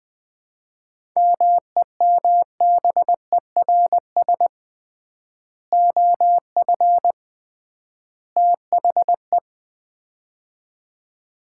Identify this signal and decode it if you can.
Morse code, heard as a keyed tone, about 20 wpm, 700 Hz. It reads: MEMBERS OF THE
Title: 3x Word-Spacing - Sets of 3 Words - 20wpm